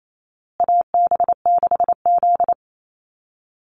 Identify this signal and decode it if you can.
Morse code, heard as a keyed tone, about 28 wpm, 700 Hz. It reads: AB6Z
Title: Call Signs Repeated 2x - 28wpm